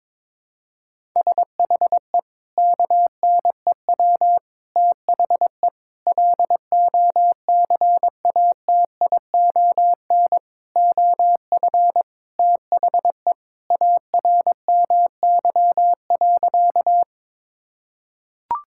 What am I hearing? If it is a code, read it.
Morse code, heard as a keyed tone, about 22 wpm, 700 Hz. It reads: SHE KNEW THE LOCATION OF THE ARMY.
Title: Sentences from Top 1000 Words - Review Part 6 - 22wpm